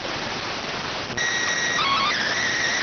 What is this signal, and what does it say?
SSTV (slow-scan television), PD 290.